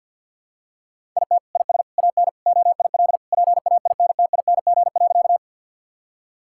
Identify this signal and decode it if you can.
Morse code, heard as a keyed tone, about 52 wpm, 700 Hz. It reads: IT IS AN OIL PAINTING.
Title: Sentences from Top 1000 Words - Review Part 6 - 50wpm